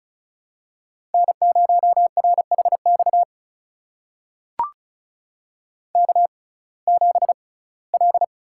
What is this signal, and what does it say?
Morse code, heard as a keyed tone, about 36 wpm, spaced as for 21 wpm, 700 Hz. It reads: N0RHX K7L
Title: USA Call Signs - 5 Characters - 6x Character Spacing - 35wpm